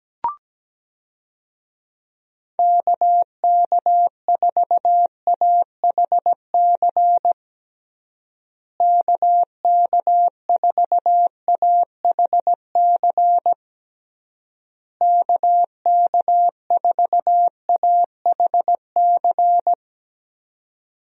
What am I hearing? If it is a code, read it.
Morse code, heard as a keyed tone, about 17 wpm, 700 Hz. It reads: KK4AHC KK4AHC KK4AHC
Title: USA Call Signs - 6 Characters - Repeated 3x - 17wpm